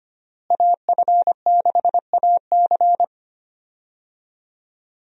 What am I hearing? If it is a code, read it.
Morse code, heard as a keyed tone, about 25 wpm, 700 Hz. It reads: AF6AC
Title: USA Call Signs - 5 Characters - Repeated 3x - 25wpm